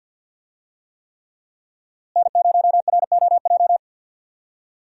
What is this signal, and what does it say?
Morse code, heard as a keyed tone, about 52 wpm, 700 Hz. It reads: N0ROJ
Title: USA Call Signs - 5 Characters - Repeated 3x - 50wpm